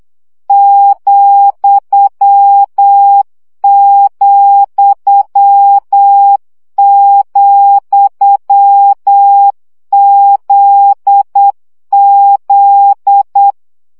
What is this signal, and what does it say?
Morse code, heard as a keyed tone, about 8 wpm, 800 Hz. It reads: ,,,ZZ
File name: CW gyakorlas z 7 vesszo 7 baud